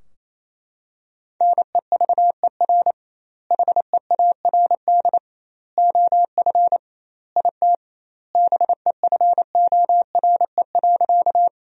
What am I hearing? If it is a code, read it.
Morse code, heard as a keyed tone, about 28 wpm, 700 Hz. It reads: NEVER HEARD OF IT BEFORE.
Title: 2x Word-Spacing - Sentences from Top 400 Words - 28wpm